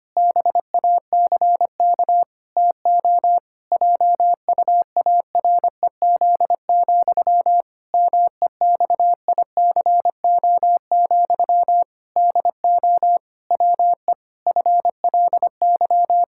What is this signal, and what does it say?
Morse code, heard as a keyed tone, about 25 wpm, 700 Hz. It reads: BACK TO JUAREZ, MEXICO, DO WE FLY
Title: Pangram Challenge v2 - 25wpm